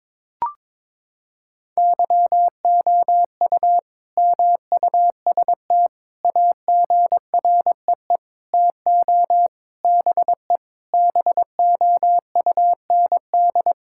Morse code, heard as a keyed tone, about 22 wpm, 700 Hz. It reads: YOU MUST AGREE TO BE BOUND
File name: Sentences with 9-Word Phrases - 22wpm